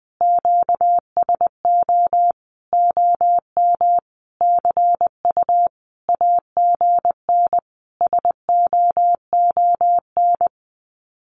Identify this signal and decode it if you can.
Morse code, heard as a keyed tone, about 20 wpm, 700 Hz. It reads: QSO OM CU AGN SOON
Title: Common QSO Phrases 20wpm